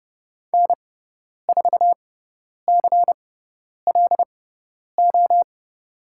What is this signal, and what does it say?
Morse code, heard as a keyed tone, about 30 wpm, spaced as for 10 wpm, 700 Hz. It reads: N4CLO
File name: Lesson 16g - Letter c - Call Signs 30wpm